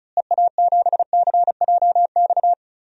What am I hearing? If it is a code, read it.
Morse code, heard as a keyed tone, about 34 wpm, 700 Hz. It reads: EA7CJX
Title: DXCC-100 Call Signs - 6 Characters - Repeated 3x - 35wpm